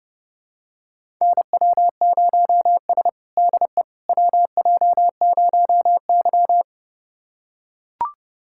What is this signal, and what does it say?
Morse code, heard as a keyed tone, about 30 wpm, 700 Hz. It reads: NW0S DE WJ0Y
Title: USA Callsign DE Callsign - 4 Characters - 30wpm